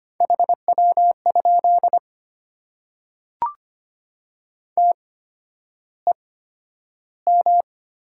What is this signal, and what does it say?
Morse code, heard as a keyed tone, about 25 wpm, spaced as for 11 wpm, 700 Hz. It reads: HW? TEM
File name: Top 90 QSO Elements - 8x Character Spacing - 25wpm